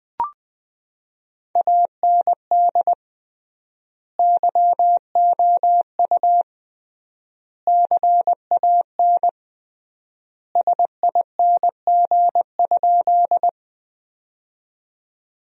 Morse code, heard as a keyed tone, about 20 wpm, 700 Hz. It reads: AND YOU CAN SING?
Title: 3x Word-Spacing - Sentences from Top 400 Words - 20wpm